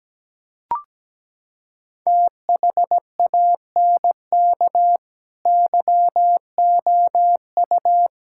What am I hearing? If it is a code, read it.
Morse code, heard as a keyed tone, about 17 wpm, 700 Hz. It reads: THANK YOU
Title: Sentences from Top 1000 Words - Review Part 6 - 17wpm